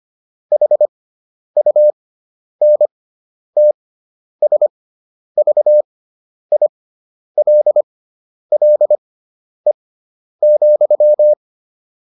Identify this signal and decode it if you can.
Morse code, heard as a keyed tone, about 25 wpm, spaced as for 10 wpm, 600 Hz. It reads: HUNTSVILLE,